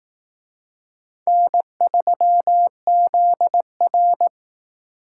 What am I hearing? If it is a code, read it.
Morse code, heard as a keyed tone, about 18 wpm, 700 Hz. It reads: N3ZR
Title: Speed Racing - USA 1x2 Call Signs 15wpm